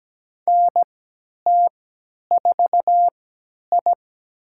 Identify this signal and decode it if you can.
Morse code, heard as a keyed tone, about 17 wpm, spaced as for 10 wpm, 700 Hz. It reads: NT4I
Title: USA Call Signs - 4 Characters - 3x Character Spacing - 17wpm